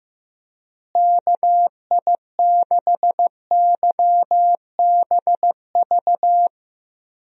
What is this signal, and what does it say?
Morse code, heard as a keyed tone, about 15 wpm, 700 Hz. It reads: KI6YBV
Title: USA Call Signs - 6 Characters - Repeated 3x - 15wpm